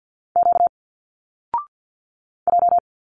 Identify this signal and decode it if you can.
Morse code, heard as a keyed tone, about 49 wpm, 700 Hz. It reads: Q J
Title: Single Letters 50wpm